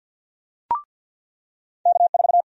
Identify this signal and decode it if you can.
Morse code, heard as a keyed tone, about 53 wpm, 700 Hz. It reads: K4
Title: USA Call Sign Prefixes - 50wpm